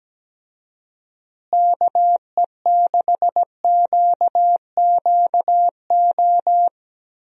Morse code, heard as a keyed tone, about 17 wpm, 700 Hz. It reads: KE6QQO
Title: USA Call Signs - 6 Characters - Repeated 3x - 17wpm